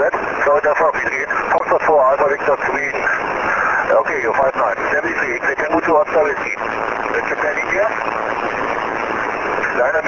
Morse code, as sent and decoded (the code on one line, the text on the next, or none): none